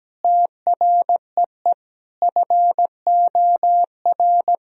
Morse code, heard as a keyed tone, about 17 wpm, 700 Hz. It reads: TREE FOR
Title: Sentences from Top 1000 Words - Review Part 1 - 17wpm